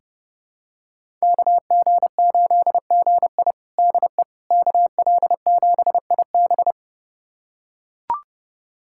Morse code, heard as a keyed tone, about 32 wpm, 700 Hz. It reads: KG8GI DE KL7IB
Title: USA Callsign DE Callsign 30wpm